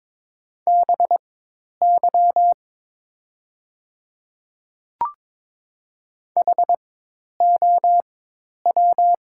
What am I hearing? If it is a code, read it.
Morse code, heard as a keyed tone, about 21 wpm, spaced as for 10 wpm, 700 Hz. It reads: BY HOW
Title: 4x Character Spacing - Top 100 Words - 22wpm